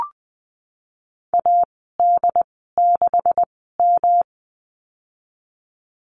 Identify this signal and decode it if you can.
Morse code, heard as a keyed tone, about 20 wpm, spaced as for 15 wpm, 700 Hz. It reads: AD6M
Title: 147 Lesson 24g - Letter y - Call Signs 20wpm